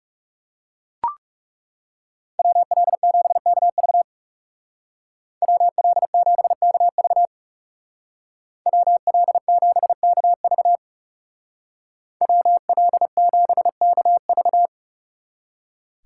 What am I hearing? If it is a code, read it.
Morse code, heard as a keyed tone, about 36 wpm, 700 Hz. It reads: WL7KV WL7KV WL7KV WL7KV
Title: Speed Racing - USA 2x2 Call Signs 30wpm